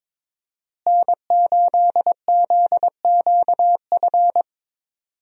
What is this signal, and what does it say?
Morse code, heard as a keyed tone, about 22 wpm, 700 Hz. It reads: N8ZQF
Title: USA Call Signs - 5 Characters - Repeated 3x - 22wpm